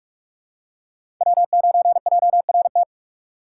Morse code, heard as a keyed tone, about 46 wpm, 700 Hz. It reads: W9JRT
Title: USA Call Signs - 5 Characters - 4x Character Spacing - 45wpm